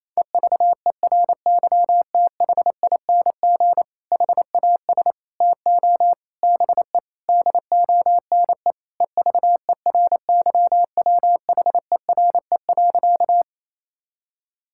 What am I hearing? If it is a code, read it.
Morse code, heard as a keyed tone, about 28 wpm, 700 Hz. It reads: EVERYTHING HAS TO BE DONE EVERYWHERE.
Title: Sentences from Top 1600 Words 28wpm